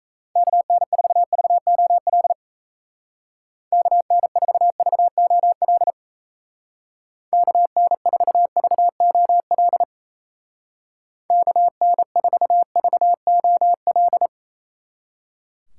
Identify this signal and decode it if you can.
Morse code, heard as a keyed tone, about 33 wpm, 700 Hz. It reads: KN4VOL KN4VOL KN4VOL KN4VOL
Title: Speed Racing - USA 2x3 Call Signs 28wpm